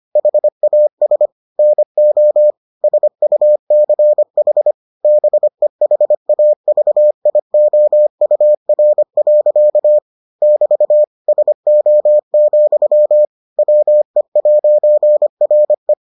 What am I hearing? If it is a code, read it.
Morse code, heard as a keyed tone, about 25 wpm, 600 Hz. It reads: HAS NO SUCH BEHAVIOUR. <BT> SO, WE'RE